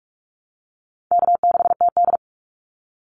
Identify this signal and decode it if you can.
Morse code, heard as a keyed tone, about 46 wpm, 700 Hz. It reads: K6TD
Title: Easy Call Signs - 45wpm